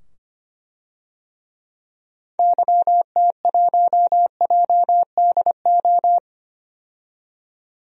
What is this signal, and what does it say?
Morse code, heard as a keyed tone, about 25 wpm, 700 Hz. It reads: YT1JDO
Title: DXCC-100 Call Signs - 6 Characters - 25wpm